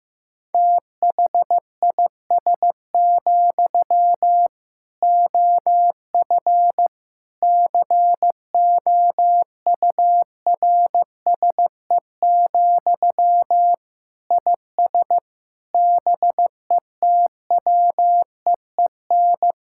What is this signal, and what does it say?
Morse code, heard as a keyed tone, about 15 wpm, 700 Hz. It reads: THIS, OF COURSE, IS BETWEEN
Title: Sentences from Top 1600 Words 15wpm